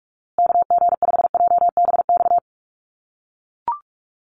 Morse code, heard as a keyed tone, about 44 wpm, 700 Hz. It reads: KG5JBX
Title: USA Call Signs Repeated 3x 45wpm